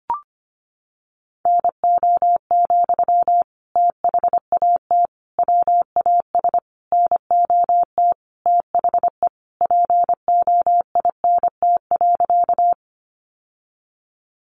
Morse code, heard as a keyed tone, about 25 wpm, 700 Hz. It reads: NO, THAT WAS NOT THE POINT.
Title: Sentences from Top 600 Words 25wpm